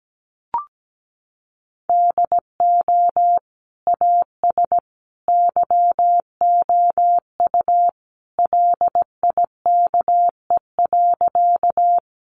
Morse code, heard as a keyed tone, about 17 wpm, 700 Hz. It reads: DO AS YOU LIKE.
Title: Sentences from Top 100 Words 17wpm